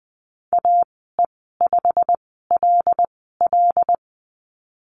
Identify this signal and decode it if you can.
Morse code, heard as a keyed tone, about 20 wpm, spaced as for 15 wpm, 700 Hz. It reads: AE5LL
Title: 161 Lesson 26g - Letter g - Call Signs 20wpm